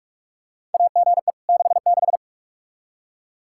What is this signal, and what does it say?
Morse code, heard as a keyed tone, about 44 wpm, 700 Hz. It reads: AGE 66
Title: K1USN SST QSO Practice 45wpm